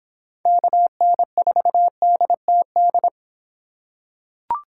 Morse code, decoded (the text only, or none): KN4DTD